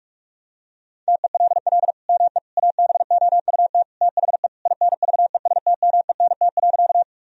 Morse code, heard as a keyed tone, about 41 wpm, 700 Hz. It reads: TELL ME ABOUT THE INVESTMENT.
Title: Sentences from Top 1000 Words - Review Part 4 - 45wpm